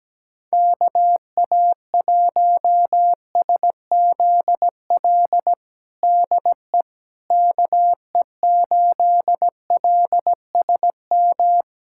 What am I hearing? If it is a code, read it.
Morse code, heard as a keyed tone, about 17 wpm, 700 Hz. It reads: KA1SZL DE KE8LSM
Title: USA Callsign DE Callsign - 6 Characters - 17wpm